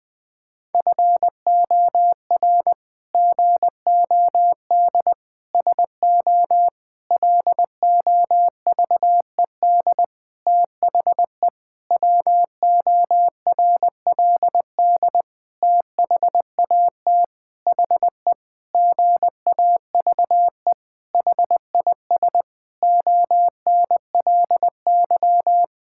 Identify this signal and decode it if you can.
Morse code, heard as a keyed tone, about 20 wpm, 700 Hz. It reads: FOR GOD SO LOVED THE WORLD THAT HE GAVE HIS ONLY
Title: Sentences with 8-Word Phrases - Spoken Beginning - 20wpm